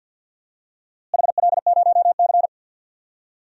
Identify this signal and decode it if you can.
Morse code, heard as a keyed tone, about 48 wpm, 700 Hz. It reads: HL0C
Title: DXCC-100 Call Signs - 4 Characters - Repeated 2x - 50wpm